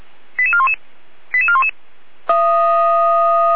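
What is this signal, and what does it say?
Five-tone selective call (ZVEI), heard as quick runs of five tones, tones of about 70 ms. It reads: 88411 88411
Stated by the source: ZVEI five-tone selective call received on an SDR.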